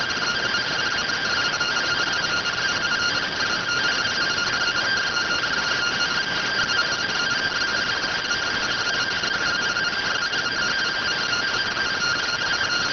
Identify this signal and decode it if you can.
Two-tone teleprinter signal, about 170 Hz shift.